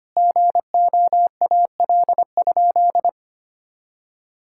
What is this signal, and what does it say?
Morse code, heard as a keyed tone, about 25 wpm, 700 Hz. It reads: GOAL?
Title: Sentences from Top 1000 Words - Review Part 6 - 25wpm